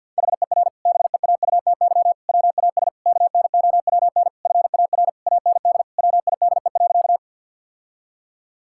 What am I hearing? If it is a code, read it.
Morse code, heard as a keyed tone, about 50 wpm, 700 Hz. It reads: HER BEAUTY WAS KNOWN FAR AND WIDE.
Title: Sentences from Top 1000 Words - Review Part 6 - 50wpm